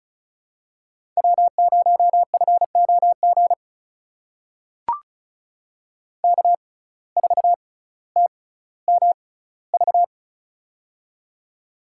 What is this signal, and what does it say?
Morse code, heard as a keyed tone, about 36 wpm, spaced as for 18 wpm, 700 Hz. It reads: W0FOG K4TMV
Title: USA Call Signs - 5 Characters - 6x Character Spacing - 35wpm